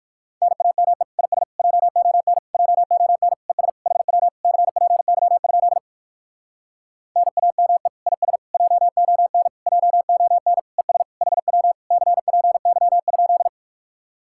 Morse code, heard as a keyed tone, about 67 wpm, spaced as for 57 wpm, 700 Hz. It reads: NAME IS JON JON ES HW CPY? NAME IS JON JON ES HW CPY?